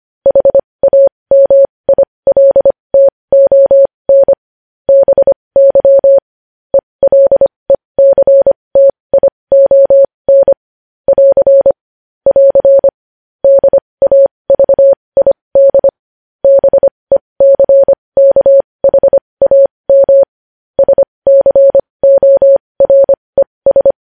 Morse code, heard as a keyed tone, about 25 wpm, spaced as for 20 wpm, 550 Hz. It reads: HAMILTON BY ELECTION <AR> <AR> DAVID BECKHAM SCORES
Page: News Headlines in Morse Code at 20 WPM